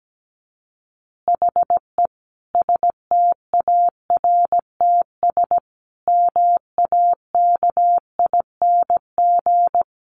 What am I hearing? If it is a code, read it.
Morse code, heard as a keyed tone, about 17 wpm, 700 Hz. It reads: HE STARTS MAKING
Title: Additional Easy Phrases 17wpm